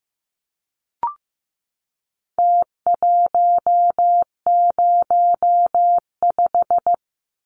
Morse code, heard as a keyed tone, about 15 wpm, 700 Hz. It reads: T105
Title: Numbers - Contest Style - 4-Digit 15wpm